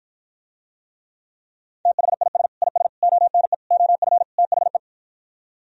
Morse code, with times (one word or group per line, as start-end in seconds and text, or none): 1.9-2.5 THIS
2.6-2.9 IS
3.0-3.5 ONE
3.7-4.2 OF
4.4-4.8 THE